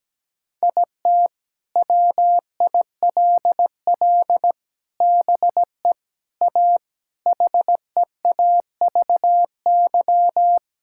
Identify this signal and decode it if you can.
Morse code, heard as a keyed tone, about 17 wpm, 700 Hz. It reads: IT WILL BE A HEAVY
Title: Sentences from Top 1600 Words 17wpm